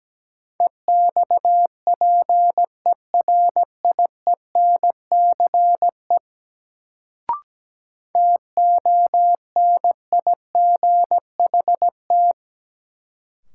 Morse code, heard as a keyed tone, about 17 wpm, 700 Hz. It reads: EXPERIENCE TONIGHT
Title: Sentences from Top 600 Words - New Word Drill - 17wpm